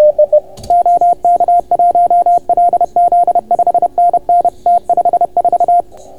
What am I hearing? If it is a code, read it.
Morse code, heard as a keyed tone, about 30 wpm, 690 Hz. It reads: OK1LZ 5NN T54